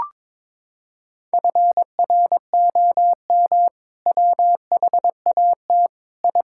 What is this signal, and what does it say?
Morse code, heard as a keyed tone, about 22 wpm, 700 Hz. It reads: FROM WHAT I
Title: Sentences from Top 1000 Words - Review Part 6 - 22wpm